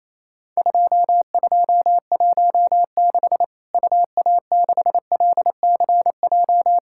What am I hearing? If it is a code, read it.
Morse code, heard as a keyed tone, about 28 wpm, 700 Hz. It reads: 2216 UA6LCJ
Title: Worked All Europe DX Contest - QTC Exchange - 28wpm